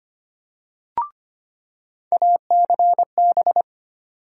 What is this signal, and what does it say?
Morse code, heard as a keyed tone, about 25 wpm, 700 Hz. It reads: ACB
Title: USA Call Sign Postfixes - 3 Letters 25wpm